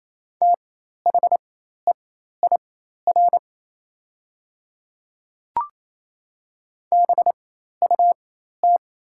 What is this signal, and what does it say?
Morse code, heard as a keyed tone, about 28 wpm, spaced as for 13 wpm, 700 Hz. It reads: THEIR BUT
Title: 4x Character Spacing - Top 100 Words - 28wpm